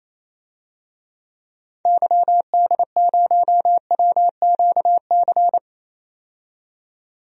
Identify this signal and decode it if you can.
Morse code, heard as a keyed tone, about 28 wpm, 700 Hz. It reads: YD0WQC
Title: DXCC-100 Call Signs - 6 Characters - 28wpm